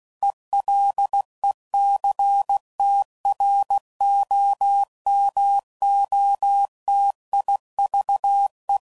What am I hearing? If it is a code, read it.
Morse code, heard as a keyed tone, about 16 wpm, 800 Hz. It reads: ELECTROMOTIVE
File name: CW Code Groups @15WPM / Technical Subjects / XL